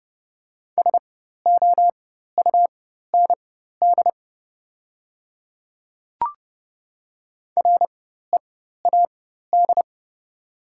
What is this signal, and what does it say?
Morse code, heard as a keyed tone, about 30 wpm, spaced as for 14 wpm, 690 Hz. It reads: SOUND READ
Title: 4x Character Spacing - Top 200 Words - 30wpm